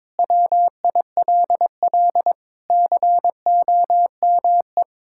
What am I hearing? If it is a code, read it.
Morse code, heard as a keyed tone, about 22 wpm, 700 Hz. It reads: WILL COME
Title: Sentences from Top 1000 Words - Review Part 6 - 22wpm